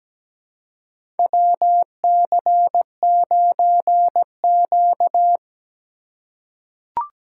Morse code, heard as a keyed tone, about 17 wpm, 700 Hz. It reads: WC9Q
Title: USA Call Signs - 4 Characters - 2x Character Spacing - 17wpm